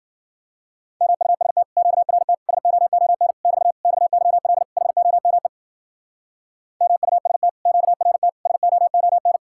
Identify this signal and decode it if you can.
Morse code, heard as a keyed tone, about 48 wpm, 700 Hz. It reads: MUST QRT SOON <BT> XYL HOME MUST QRT SOON
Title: Speed-Racing Common QSO Phrases - 40wpm